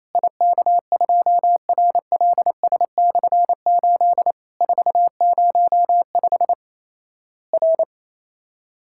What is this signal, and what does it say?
Morse code, heard as a keyed tone, about 28 wpm, 700 Hz. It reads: IK2RLS/8 405 R
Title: Worked All Europe DX Contest - QTC Exchange - 28wpm